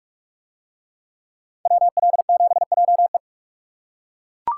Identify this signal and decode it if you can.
Morse code, heard as a keyed tone, about 41 wpm, 700 Hz. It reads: WL7JE